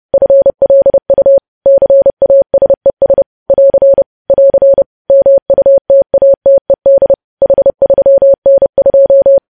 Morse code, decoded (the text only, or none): FLU CASES <AR> <AR> MUTATED H3N2